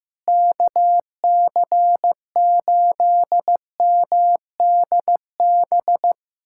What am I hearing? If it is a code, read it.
Morse code, heard as a keyed tone, about 16 wpm, 700 Hz. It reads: KC8MDB
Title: USA Callsign DE Callsign 15wpm